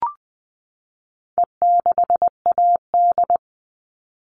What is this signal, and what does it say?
Morse code, heard as a keyed tone, about 20 wpm, 700 Hz. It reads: E6AD
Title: All Cut Numbers - 4-Digit 20wpm